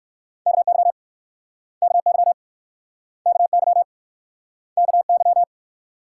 Morse code, heard as a keyed tone, about 58 wpm, 700 Hz. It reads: KY KY KY KY